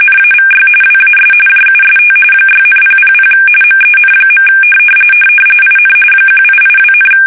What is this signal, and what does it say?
RTTY, about 45.45 baud. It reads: NAVY WAS JUST 6,000 MEN. IN 1882,
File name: fsk850 / usnavy1